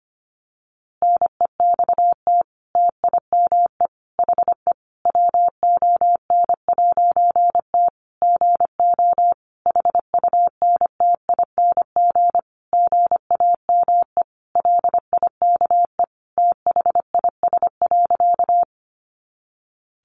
Morse code, heard as a keyed tone, about 25 wpm, 700 Hz. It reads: NEXT TIME HE WON'T GO HUNTING GAME LIKE THIS.
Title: A Sound Of Thunder 25wpm